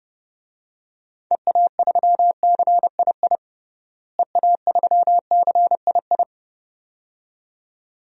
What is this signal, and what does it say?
Morse code, heard as a keyed tone, about 30 wpm, 700 Hz. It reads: EA3CII EA3CII
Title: DXCC-100 Call Signs - 6 Characters - Repeated 2x - 30wpm